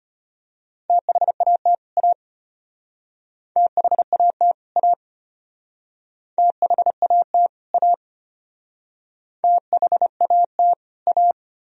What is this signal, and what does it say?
Morse code, heard as a keyed tone, about 33 wpm, 700 Hz. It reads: THAT A THAT A THAT A THAT A